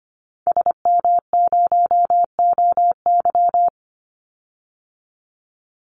Morse code, heard as a keyed tone, about 25 wpm, 700 Hz. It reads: SM0OY
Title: Call Signs Repeated 2x - 25wpm